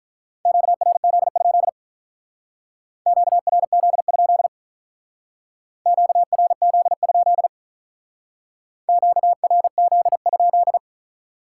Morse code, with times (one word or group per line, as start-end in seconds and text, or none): 0.5-1.7 QRZ?
3.1-4.5 QRZ?
5.9-7.5 QRZ?
8.9-10.8 QRZ?